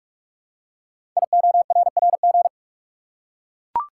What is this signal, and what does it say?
Morse code, heard as a keyed tone, about 44 wpm, 700 Hz. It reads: IORRG
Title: Code Groups - 5 Letters - 45wpm